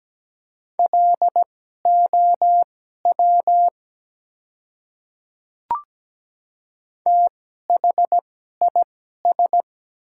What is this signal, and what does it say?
Morse code, heard as a keyed tone, about 17 wpm, spaced as for 12 wpm, 690 Hz. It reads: LOW THIS
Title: 2x Character Spacing - Top 300 Words - 17wpm